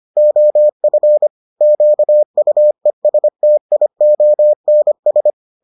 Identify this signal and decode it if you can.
Morse code, heard as a keyed tone, about 25 wpm, 600 Hz. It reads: OF QUESTIONS